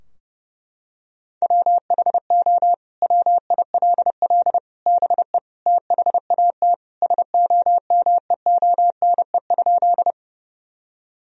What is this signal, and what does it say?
Morse code, heard as a keyed tone, about 30 wpm, 700 Hz. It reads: WHO WILL BE THAT SOMEONE?
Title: Sentences from Top 1000 Words - Review Part 4 - 30wpm